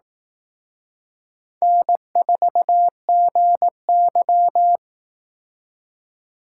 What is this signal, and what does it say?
Morse code, heard as a keyed tone, about 18 wpm, 700 Hz. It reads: N4GY
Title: Speed Racing - USA 1x2 Call Signs 15wpm